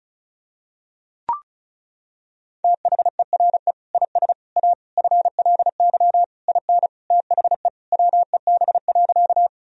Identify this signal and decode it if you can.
Morse code, heard as a keyed tone, about 36 wpm, 700 Hz. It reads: THERE IS A FLY IN THE WEB.
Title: Sentences from Top 1000 Words - Review Part 4 - 35wpm